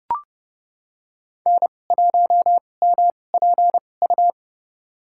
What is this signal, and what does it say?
Morse code, heard as a keyed tone, about 31 wpm, spaced as for 22 wpm, 700 Hz. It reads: N1MPU
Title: USA Call Signs - 5 Characters - 2x Character Spacing - 30wpm